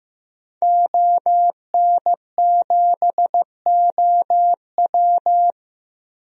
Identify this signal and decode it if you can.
Morse code, heard as a keyed tone, about 15 wpm, 700 Hz. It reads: ON7OW
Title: DXCC-100 Call Signs - 5 Characters - Repeated 3x - 15wpm